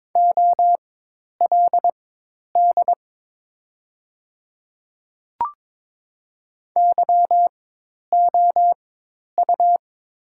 Morse code, heard as a keyed tone, about 22 wpm, spaced as for 10 wpm, 690 Hz. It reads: OLD YOU
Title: 4x Character Spacing - Top 200 Words - 22wpm